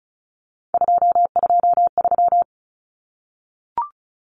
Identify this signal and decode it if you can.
Morse code, heard as a keyed tone, about 34 wpm, 700 Hz. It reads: 223
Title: Common Cut Numbers - 3-Digit 35wpm